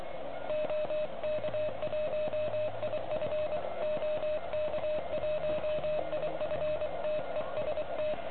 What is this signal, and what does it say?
Morse code, heard as a keyed tone, about 25 wpm, 610 Hz, with noise. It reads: OK1IF OK1IF TEST
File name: ARCHIV / OK QRP 2021 / REC11